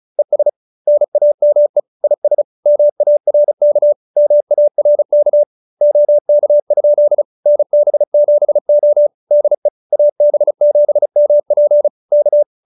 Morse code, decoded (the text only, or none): ES NAME IS MARK MARK OK? NB7O DE AB7MP K